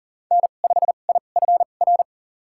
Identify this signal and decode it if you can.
Morse code, heard as a keyed tone, about 40 wpm, spaced as for 29 wpm, 700 Hz. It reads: N5IFR
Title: USA Call Signs - 2x Character Spacing - 40wpm